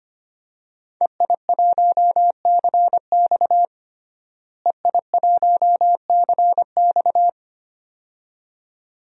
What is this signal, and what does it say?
Morse code, heard as a keyed tone, about 25 wpm, 700 Hz. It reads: EI1CX EI1CX
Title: DXCC-100 Call Signs - 5 Characters - Repeated 2x - 25wpm